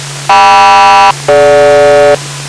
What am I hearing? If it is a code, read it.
Aircraft SELCAL, heard as two pairs of tones. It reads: KM-DG